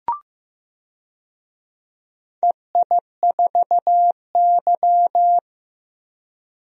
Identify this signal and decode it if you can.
Morse code, heard as a keyed tone, about 16 wpm, 700 Hz. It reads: EI4Y
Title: DXCC-100 Call Signs - 4 Characters - Repeated 2x - 15wpm